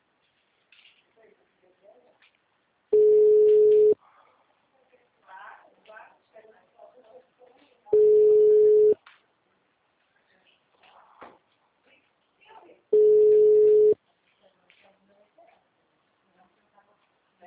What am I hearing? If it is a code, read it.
Ringback tone (Europe, 425 Hz), with speech.
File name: out- -2016- - - .13370